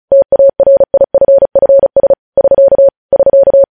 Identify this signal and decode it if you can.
Morse code, heard as a keyed tone, about 36 wpm, 550 Hz. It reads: TARIFFS <SK> <SK>
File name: podcast / News headlines-1x-35x35